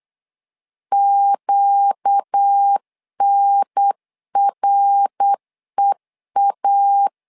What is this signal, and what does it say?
Morse code, heard as a keyed tone, about 8 wpm, 800 Hz. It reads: QNREA
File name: LEZIONI / 4 Q 8wpm 2732